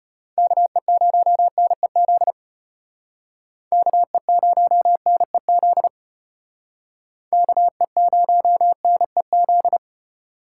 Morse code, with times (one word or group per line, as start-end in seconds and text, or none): 0.4-2.3 KE0NEZ
3.7-5.9 KE0NEZ
7.3-9.8 KE0NEZ